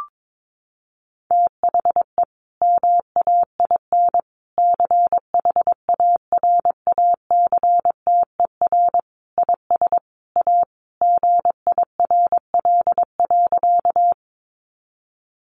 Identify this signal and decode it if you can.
Morse code, heard as a keyed tone, about 22 wpm, 700 Hz. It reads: THE MAIN CHARACTER IS A GIRL.
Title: Additional Easy Phrases 22wpm